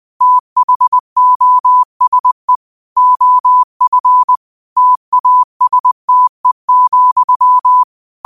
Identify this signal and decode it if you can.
Morse code, heard as a keyed tone, about 20 wpm, 1000 Hz. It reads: THOSE OF TASTE,
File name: qotdfiles / 20wpm